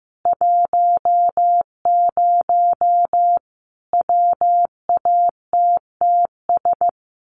Morse code, heard as a keyed tone, about 14 wpm, 700 Hz. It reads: 10 WATTS
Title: Common QSO Phrases 15wpm